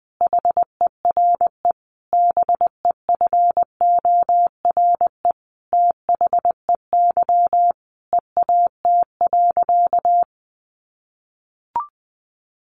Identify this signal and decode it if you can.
Morse code, heard as a keyed tone, about 20 wpm, 700 Hz. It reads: HERE BEFORE THEY EAT.
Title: Sentences from Top 700 Words 20wpm